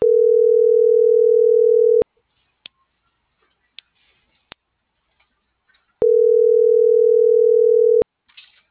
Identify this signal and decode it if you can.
Ringback tone (North America), 440 + 480 Hz.